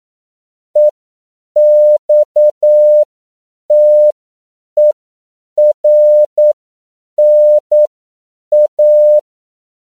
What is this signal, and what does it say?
Morse code, heard as a keyed tone, about 9 wpm, spaced as for 7 wpm, 600 Hz. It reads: EXTERNA
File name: CW / 10PPM